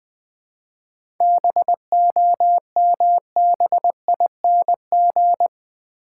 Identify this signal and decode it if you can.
Morse code, heard as a keyed tone, about 20 wpm, 700 Hz. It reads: BOMBING
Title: 7 Letter Words - First 3 Letters Given - 20wpm